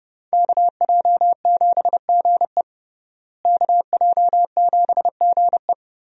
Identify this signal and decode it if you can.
Morse code, heard as a keyed tone, about 31 wpm, 700 Hz. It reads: KJ7GE KJ7GE
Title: USA Call Signs - 5 Characters - Repeated 3x - 30wpm